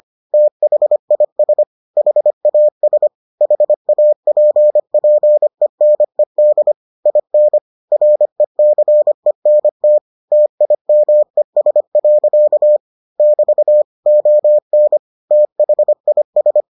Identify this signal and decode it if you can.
Morse code, heard as a keyed tone, about 25 wpm, 600 Hz. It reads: THIS HAS HAPPENED IN RECENT TIMES. <BT> ON THIS